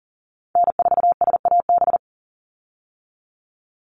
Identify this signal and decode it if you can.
Morse code, heard as a keyed tone, about 40 wpm, 700 Hz. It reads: N4SAB
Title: USA Call Signs 40wpm